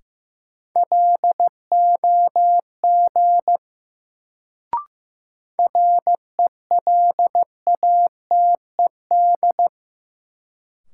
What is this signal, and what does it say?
Morse code, heard as a keyed tone, about 15 wpm, 700 Hz. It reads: LOG RELATED
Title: Sentences from Top 1400 Words - New Word Drill Alt - 15wpm